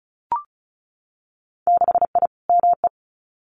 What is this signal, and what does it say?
Morse code, heard as a keyed tone, about 36 wpm, 700 Hz. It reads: 6I ME
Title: Winter Field Day Exchanges Encore 35wpm